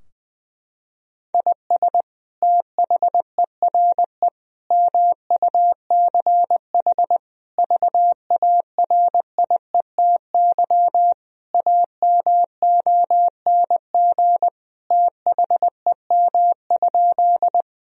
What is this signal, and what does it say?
Morse code, heard as a keyed tone, about 20 wpm, 700 Hz. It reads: IS THERE MUCH VARIETY AMONG THEM?
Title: Sentences from Top 1000 Words - Review Part 4 - 20wpm